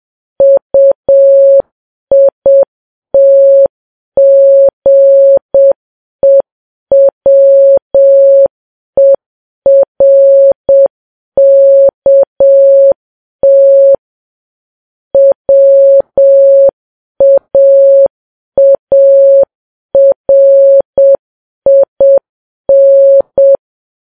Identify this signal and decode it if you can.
Morse code, heard as a keyed tone, about 7 wpm, 550 Hz. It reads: UITGEWERKT WAARIN
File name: Files Web tls / Morse on9fv / morse7kps